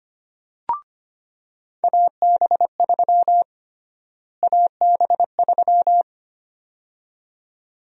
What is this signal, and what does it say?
Morse code, heard as a keyed tone, about 25 wpm, 700 Hz. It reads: AB3 AB3
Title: USA Call Sign Prefixes - Repeated 2x - 25wpm